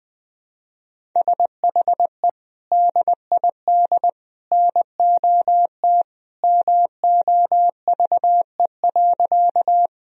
Morse code, heard as a keyed tone, about 20 wpm, 700 Hz. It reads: SHE DID NOT MOVE.
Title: Sentences from Top 1000 Words - Review Part 1 - 20wpm